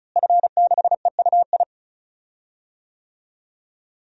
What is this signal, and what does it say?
Morse code, heard as a keyed tone, about 36 wpm, 700 Hz. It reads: F6EUI
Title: DXCC-100 Call Signs - 5 Characters - Repeated 2x - 35wpm